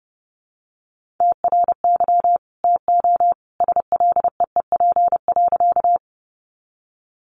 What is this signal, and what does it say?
Morse code, heard as a keyed tone, about 30 wpm, 700 Hz. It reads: TRY TO SLEEP.
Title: Sentences from Top 500 Words 30wpm